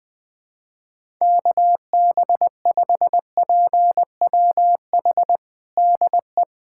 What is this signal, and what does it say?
Morse code, heard as a keyed tone, about 20 wpm, 700 Hz. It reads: KB5PWH DE
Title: USA Callsign DE Callsign 20wpm